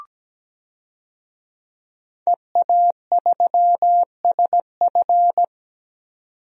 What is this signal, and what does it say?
Morse code, heard as a keyed tone, about 17 wpm, 700 Hz. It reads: EA3SF
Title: DXCC-100 Call Signs - 5 Characters - 17wpm